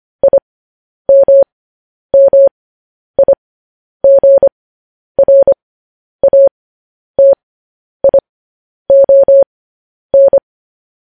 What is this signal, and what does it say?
Morse code, heard as a keyed tone, about 25 wpm, spaced as for 10 wpm, 550 Hz. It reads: IMMIGRATION
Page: News Headlines in Morse Code at 10 WPM